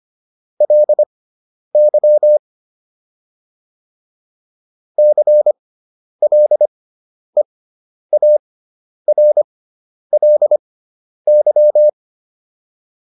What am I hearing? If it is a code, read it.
Morse code, heard as a keyed tone, about 25 wpm, spaced as for 10 wpm, 600 Hz. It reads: LY CLEARLY
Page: fundamental-practice-files